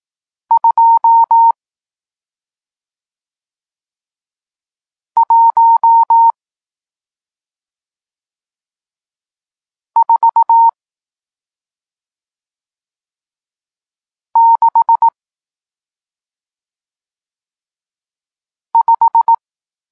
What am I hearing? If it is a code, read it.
Morse code, heard as a keyed tone, about 18 wpm, 920 Hz. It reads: 2 1 4 6 5
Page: Tekst 40 – Nauka znaków przestankowych